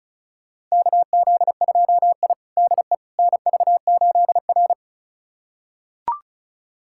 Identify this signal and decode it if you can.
Morse code, heard as a keyed tone, about 36 wpm, 690 Hz. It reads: KZ2I DE NV8R
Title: USA Callsign DE Callsign - 4 Characters - 35wpm